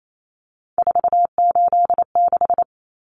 Morse code, heard as a keyed tone, about 28 wpm, 700 Hz. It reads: 486
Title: Common Cut Numbers - 3-Digit 28wpm